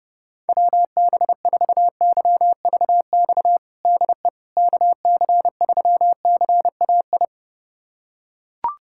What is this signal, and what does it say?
Morse code, heard as a keyed tone, about 32 wpm, 700 Hz. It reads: WB4YVX DE KC3CAI
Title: USA Callsign DE Callsign - 6 Characters - 30wpm